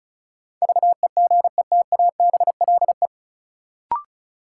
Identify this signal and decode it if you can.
Morse code, heard as a keyed tone, about 36 wpm, 700 Hz. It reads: VEGETABLE
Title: Sentences from Top 1300 Words - New Word Drill - 35wpm